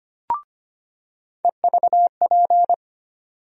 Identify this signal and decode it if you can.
Morse code, heard as a keyed tone, about 25 wpm, 700 Hz. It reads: EVP
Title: USA Call Sign Postfixes - 3 Letters 25wpm